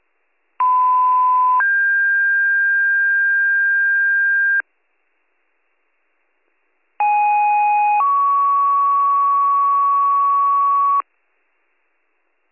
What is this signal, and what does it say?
Two-tone page, heard as a one-second tone, then a three-second tone, about 1010 then 1600 Hz, about 830 then 1120 Hz. It is heard from 0.6 to 11.0 s.